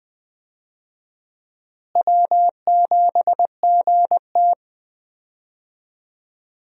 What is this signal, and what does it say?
Morse code, heard as a keyed tone, about 20 wpm, 700 Hz. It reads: W7GT
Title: USA Call Signs - 4 Characters - Repeated 3x - 20wpm